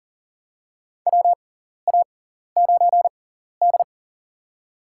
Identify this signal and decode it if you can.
Morse code, heard as a keyed tone, about 40 wpm, spaced as for 14 wpm, 700 Hz. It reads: WA9D
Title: USA Call Signs - 6x Character Spacing - 40wpm